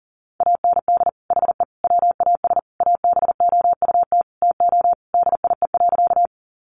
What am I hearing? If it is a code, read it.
Morse code, heard as a keyed tone, about 40 wpm, 700 Hz. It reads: AND HE WAS ABOUT TO DIE.
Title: Sentences from Top 1300 Words 40wpm